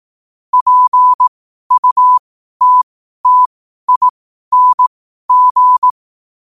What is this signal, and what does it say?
Morse code, heard as a keyed tone, about 18 wpm, spaced as for 12 wpm, 1000 Hz. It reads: PUTTING
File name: qotdfiles / 12wpm